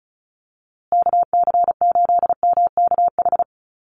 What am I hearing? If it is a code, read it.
Morse code, heard as a keyed tone, about 34 wpm, 700 Hz. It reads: KC8MKH
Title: USA Call Signs Repeated 2x 35wpm